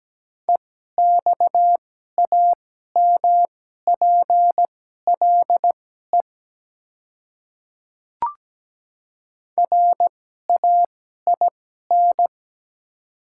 Morse code, heard as a keyed tone, about 17 wpm, spaced as for 12 wpm, 700 Hz. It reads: EXAMPLE RAIN
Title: 2x Character Spacing - Top 300 Words - 17wpm